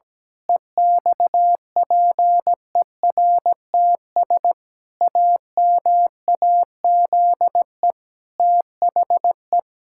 Morse code, heard as a keyed tone, about 17 wpm, 700 Hz. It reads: EXPERTS AMAZE THE
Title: Pangram Challenge v2 - 17wpm